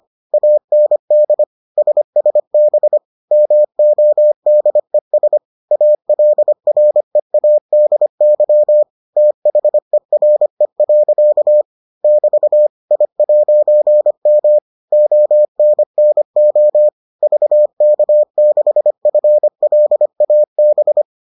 Morse code, heard as a keyed tone, about 25 wpm, 600 Hz. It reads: AND SSB MODES ALREADY THERE. <BT> I'M ONNO VK6FLAB